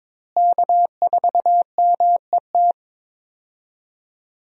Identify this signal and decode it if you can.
Morse code, heard as a keyed tone, about 21 wpm, 700 Hz. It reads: K4MET
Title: USA Call Signs - 5 Characters - Repeated 2x - 22wpm